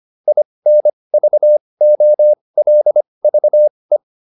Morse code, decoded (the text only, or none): INVOLVE